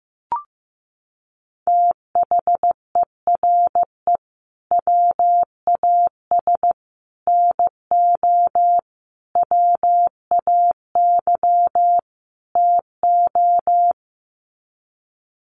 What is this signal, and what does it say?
Morse code, heard as a keyed tone, about 15 wpm, 700 Hz. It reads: THERE WAS NO WAY TO
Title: Sets of 5 Words Encore 15wpm